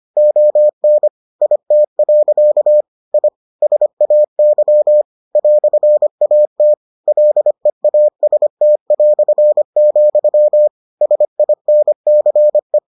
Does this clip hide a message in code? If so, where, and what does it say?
Yes: Morse code from 0.2 to 12.8 s, heard as a keyed tone, about 25 wpm, 600 Hz. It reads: ON IT. I SAY "AT LEAST", SINCE